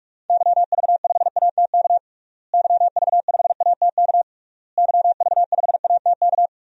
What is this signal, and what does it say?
Morse code, heard as a keyed tone, about 41 wpm, 700 Hz. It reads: YV5ATK YV5ATK YV5ATK
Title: DXCC-100 Call Signs - 6 Characters - Repeated 3x - 45wpm